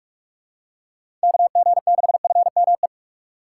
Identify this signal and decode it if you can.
Morse code, heard as a keyed tone, about 46 wpm, 700 Hz. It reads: KG6FME